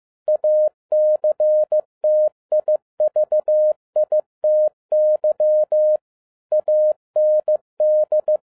Morse code, heard as a keyed tone, about 15 wpm, 610 Hz. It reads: ACTIVITY AND
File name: 2011-07-mp3 / n47 JOHN K3TN 15wpm